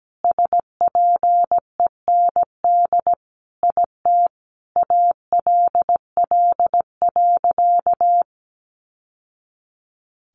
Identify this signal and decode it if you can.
Morse code, heard as a keyed tone, about 17 wpm, 700 Hz. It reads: SPEND IT ALL.
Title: Sentences from Top 800 Words 17wpm